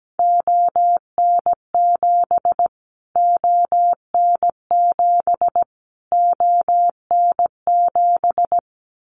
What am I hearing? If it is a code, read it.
Morse code, heard as a keyed tone, about 17 wpm, 700 Hz. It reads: ON7 ON7 ON7
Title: Callsign Prefixes for Top 25 DXCC - Repeated 3x - 17wpm